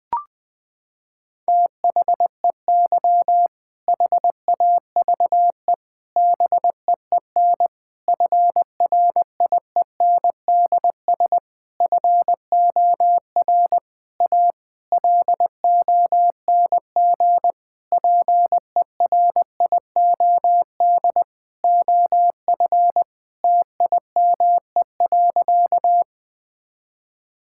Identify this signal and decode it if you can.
Morse code, heard as a keyed tone, about 20 wpm, 700 Hz. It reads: THEY HAVE BEEN FRIENDS FOR A LONG PERIOD OF TIME.
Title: Sentences with 6-Word Phrases - 20wpm